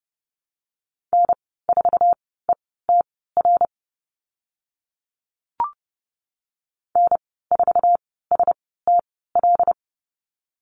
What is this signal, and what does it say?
Morse code, heard as a keyed tone, about 30 wpm, spaced as for 17 wpm, 700 Hz. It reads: N4ETR N4STL
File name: Lesson 14g - Number 2 - Call Signs 30wpm